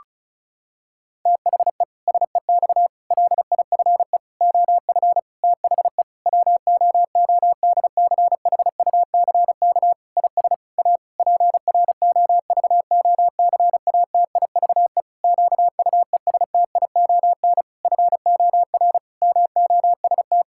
Morse code, heard as a keyed tone, about 36 wpm, 700 Hz. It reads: THE SEX LIFE OF THE WOODCHUCK IS A PROVOCATIVE QUESTION FOR MOST
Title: Pangram Challenge v2 - 35wpm